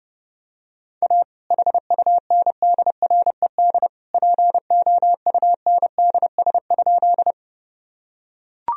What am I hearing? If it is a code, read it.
Morse code, heard as a keyed tone, about 32 wpm, 700 Hz. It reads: A HUNDRED POUNDS?
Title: Sentences from Top 1000 Words - Review Part 2 - 30wpm